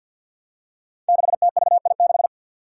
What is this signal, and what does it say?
Morse code, heard as a keyed tone, about 47 wpm, 700 Hz. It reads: 6TVI6
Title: Five Challenging Letter-Number Combinations 50wpm